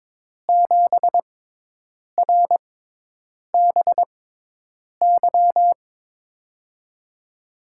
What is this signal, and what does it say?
Morse code, heard as a keyed tone, about 22 wpm, spaced as for 8 wpm, 700 Hz. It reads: 7RBY
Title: USA Call Signs - 6 Characters - 6x Character Spacing - 22wpm